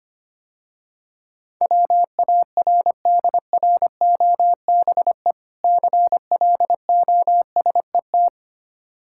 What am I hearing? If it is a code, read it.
Morse code, heard as a keyed tone, about 25 wpm, 700 Hz. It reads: WARDROBE CLOSET
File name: Difficult Words - Speed-Racing with Hints - 25wpm